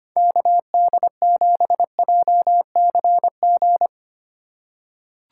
Morse code, heard as a keyed tone, about 25 wpm, 700 Hz. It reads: KD7JCG
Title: USA Call Signs - 2x Character Spacing - 25wpm